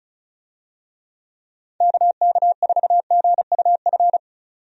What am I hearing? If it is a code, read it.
Morse code, heard as a keyed tone, about 36 wpm, 700 Hz. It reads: KK4GUF
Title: USA Call Signs - 6 Characters - 35wpm